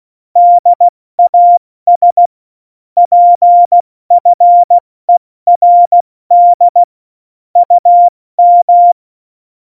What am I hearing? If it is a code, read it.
Morse code, heard as a keyed tone, about 16 wpm, 700 Hz. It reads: DAS PFERD UM
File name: Morsekurs / Witze / Witz 19. Juni 2018, 70 F80